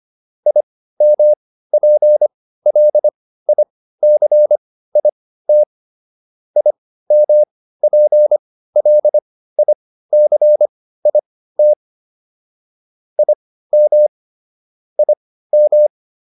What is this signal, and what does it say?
Morse code, heard as a keyed tone, about 25 wpm, spaced as for 15 wpm, 600 Hz. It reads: IMPLICIT IMPLICIT IM IM